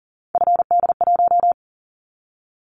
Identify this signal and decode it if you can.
Morse code, heard as a keyed tone, about 40 wpm, 700 Hz. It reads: FD1
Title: Callsign Prefixes for Top 25 DXCC - 40wpm